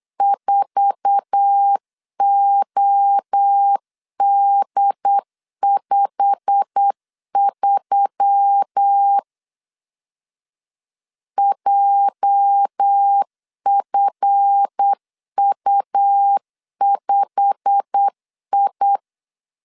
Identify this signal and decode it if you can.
Morse code, heard as a keyed tone, about 8 wpm, 800 Hz. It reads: 4OD53 JFU5I
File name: LEZIONI / 8 YJ45 / 8 YJ45 8wpm 3214